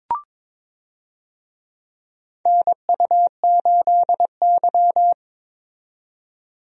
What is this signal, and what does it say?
Morse code, heard as a keyed tone, about 22 wpm, 700 Hz. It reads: NU8Y
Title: USA Call Signs - 4 Characters - 22wpm